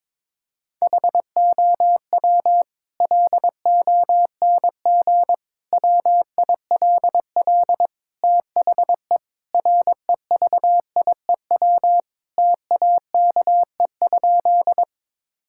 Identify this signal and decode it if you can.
Morse code, heard as a keyed tone, about 22 wpm, 700 Hz. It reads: HOW LONG WILL THE REVIEW TAKE?
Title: Sentences from Top 1000 Words - Review Part 6 - 22wpm